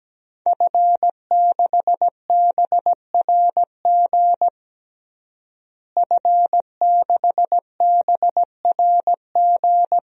Morse code, heard as a keyed tone, about 17 wpm, 700 Hz. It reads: F6BRG F6BRG
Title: DXCC-100 Call Signs - 5 Characters - Repeated 2x - 17wpm